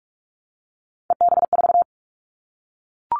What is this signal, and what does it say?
Morse code, heard as a keyed tone, about 44 wpm, 700 Hz. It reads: EB4
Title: Callsign Prefixes for Top 25 DXCC - 45wpm